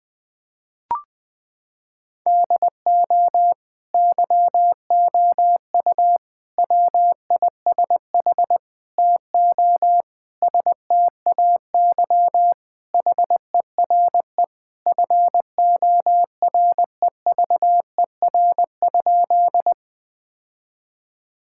Morse code, heard as a keyed tone, about 20 wpm, 700 Hz. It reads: DO YOU WISH TO STAY HERE FOREVER?
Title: Sentences from Top 1600 Words 20wpm